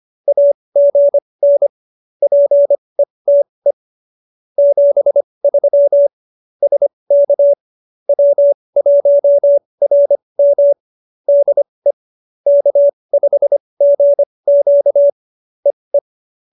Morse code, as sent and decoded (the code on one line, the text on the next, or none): .- --. -. / .--. - . / --... ...-- / ... -.- / .-- .---- .-. -- / -.. / -.- ..... --. --.- / . .
AGN PETE 73 SK W1RM DE K5GQ EE